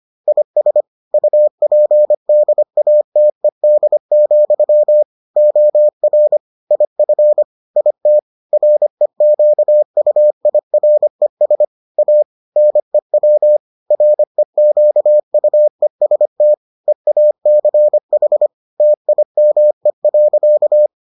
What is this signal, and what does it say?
Morse code, heard as a keyed tone, about 25 wpm, 600 Hz. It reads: IS UPDATED, OR IF IT REQUIRES A NEW REQUEST EACH TIME.